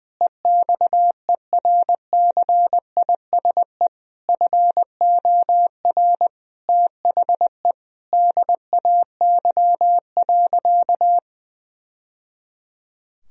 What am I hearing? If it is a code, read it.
Morse code, heard as a keyed tone, about 20 wpm, 700 Hz. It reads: EXERCISE FOR THE DAY.
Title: Sentences from Top 1000 Words - Review Part 4 - 20wpm